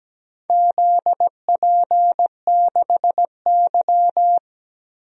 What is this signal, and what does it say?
Morse code, heard as a keyed tone, about 17 wpm, 700 Hz. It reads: ZP6Y
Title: DXCC-100 Call Signs - 4 Characters - Repeated 2x - 17wpm